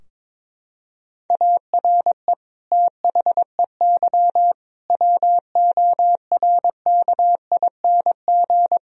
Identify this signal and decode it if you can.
Morse code, heard as a keyed tone, about 22 wpm, 700 Hz. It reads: ARE THEY WORKING
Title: Sentences from Top 1000 Words - Review Part 4 - 22wpm